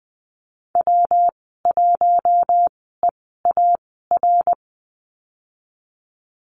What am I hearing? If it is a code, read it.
Morse code, heard as a keyed tone, about 19 wpm, spaced as for 14 wpm, 700 Hz. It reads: W1EAR
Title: 154 Lesson 25g - Letter p - Call Signs 20wpm